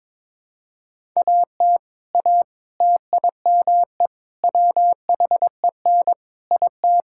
Morse code, heard as a keyed tone, about 22 wpm, 700 Hz. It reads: AT A TIME WHEN IT
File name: practice / 22wpm / Sentences with 4-Word Phrases - 22wpm